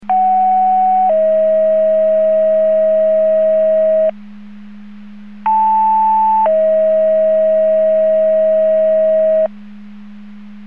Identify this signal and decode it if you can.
Two-tone page, heard as a one-second tone, then a three-second tone, about 750 then 640 Hz, about 880 then 640 Hz. It is heard from 0.1 to 9.5 s.
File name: Portage 154.385 Paging